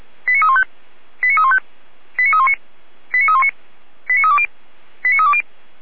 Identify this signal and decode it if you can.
Five-tone selective call (ZVEI), heard as quick runs of five tones, tones of about 70 ms. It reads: 89315 89315 89319 89319 89320 89320
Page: ZVEI